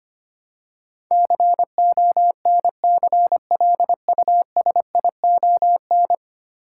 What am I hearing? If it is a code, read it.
Morse code, heard as a keyed tone, about 25 wpm, 700 Hz. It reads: CONCLUSION
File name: Ten Letter Words - 25wpm